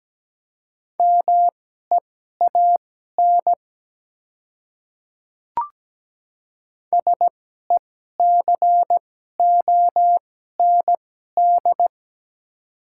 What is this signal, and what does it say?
Morse code, heard as a keyed tone, about 17 wpm, spaced as for 12 wpm, 700 Hz. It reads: MEAN SECOND
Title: 2x Character Spacing - Top 300 Words - 17wpm